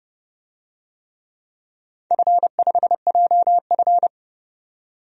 Morse code, heard as a keyed tone, about 30 wpm, 700 Hz. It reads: F5JF
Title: DXCC-100 Call Signs - 4 Characters - 30wpm